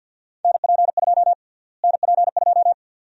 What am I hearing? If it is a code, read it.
Morse code, heard as a keyed tone, about 52 wpm, 700 Hz. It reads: NP2 NP2
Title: USA Call Sign Prefixes - Repeated 2x - 50wpm